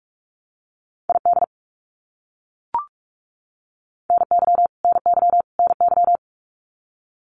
Morse code, heard as a keyed tone, about 44 wpm, 700 Hz. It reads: ID NY NY NY
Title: US States Repeated 3x 45wpm